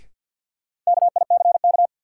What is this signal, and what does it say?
Morse code, heard as a keyed tone, about 53 wpm, 700 Hz. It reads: KICK
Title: Sentences from Top 1300 Words - New Word Drill - 50wpm